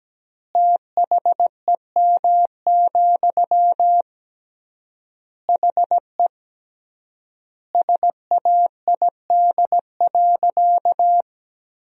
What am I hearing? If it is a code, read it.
Morse code, heard as a keyed tone, about 17 wpm, 700 Hz. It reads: THEM, HE SAID.
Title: 3x Word-Spacing - Sentences from Top 400 Words - 17wpm